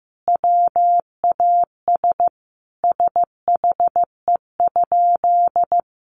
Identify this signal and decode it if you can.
Morse code, heard as a keyed tone, about 15 wpm, 700 Hz. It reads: WAS SHE?
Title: Sentences from Top 200 Words 15wpm